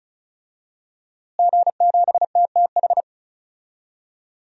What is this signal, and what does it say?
Morse code, heard as a keyed tone, about 36 wpm, 700 Hz. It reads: G7TTH
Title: DXCC-100 Call Signs - 5 Characters - Repeated 3x - 35wpm